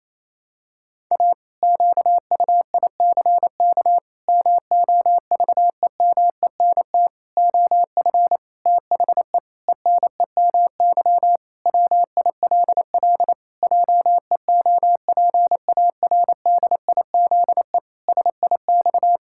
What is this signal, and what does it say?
Morse code, heard as a keyed tone, about 27 wpm, 700 Hz. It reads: A QUICK MOVEMENT OF THE ENEMY WILL JEOPARDIZE SIX
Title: Pangram Challenge v2 - 28wpm